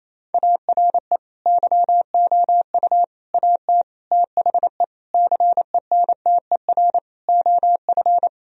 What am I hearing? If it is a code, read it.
Morse code, heard as a keyed tone, about 28 wpm, 700 Hz. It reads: ARE YOU AT THE CENTER OF
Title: Sentences from Top 1000 Words - Review Part 2 - 28wpm